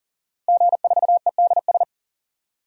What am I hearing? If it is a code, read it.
Morse code, heard as a keyed tone, about 40 wpm, 700 Hz. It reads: G4EDS
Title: DXCC-100 Call Signs - 5 Characters - Repeated 2x - 40wpm